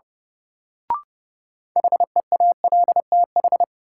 Morse code, heard as a keyed tone, about 30 wpm, 700 Hz. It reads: HEALTH
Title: Sets of 5 Words - New Word Drill - 30wpm